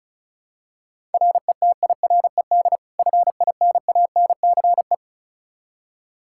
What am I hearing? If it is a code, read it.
Morse code, heard as a keyed tone, about 34 wpm, 700 Hz. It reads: RETIRED FINANCE
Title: K1USN SST QSO Practice 35wpm